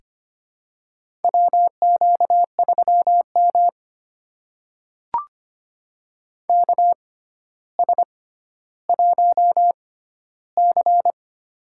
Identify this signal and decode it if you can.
Morse code, heard as a keyed tone, about 25 wpm, spaced as for 13 wpm, 700 Hz. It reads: WQ3M KS1C
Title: USA Call Signs - 4 Characters - 6x Character Spacing - 25wpm